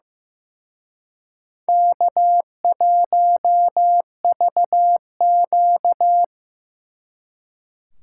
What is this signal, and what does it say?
Morse code, heard as a keyed tone, about 18 wpm, 700 Hz. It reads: K1VQ
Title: Speed Racing - USA 1x2 Call Signs 15wpm